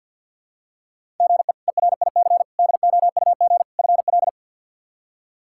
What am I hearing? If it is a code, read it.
Morse code, heard as a keyed tone, about 51 wpm, 700 Hz. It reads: GE ERIC DOUG FL
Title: K1USN SST QSO Practice - 50wpm